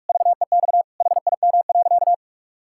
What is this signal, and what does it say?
Morse code, heard as a keyed tone, about 46 wpm, 700 Hz. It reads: VEX HIM.
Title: Pangram Challenge v2 - 45wpm